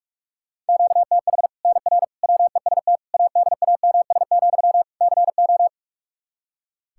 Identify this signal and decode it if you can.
Morse code, heard as a keyed tone, about 45 wpm, 700 Hz. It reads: QTH NR WEST ADAMS, CO
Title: Practice QSO - Exchange 1 - Beginning - 45wpm